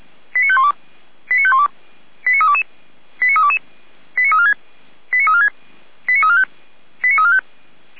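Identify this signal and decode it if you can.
Five-tone selective call (ZVEI), heard as quick runs of five tones, tones of about 70 ms. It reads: 89512 89512 89322 89322 89346 89346 89345 89345
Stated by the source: ZVEI five-tone alert received by SDR, from Feuerwehr Orlamünde's audio archive.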